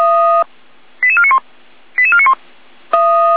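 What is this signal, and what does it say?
Five-tone selective call (ZVEI), heard as quick runs of five tones, tones of about 70 ms. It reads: 88491 88491
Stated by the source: Five-tone ZVEI selective call 88485, received by SDR.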